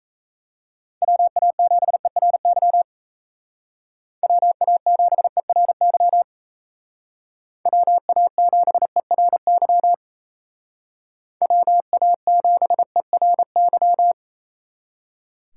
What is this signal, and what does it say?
Morse code, heard as a keyed tone, about 34 wpm, 700 Hz. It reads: WA7ERY WA7ERY WA7ERY WA7ERY
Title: Speed Racing - USA 2x3 Call Signs 28wpm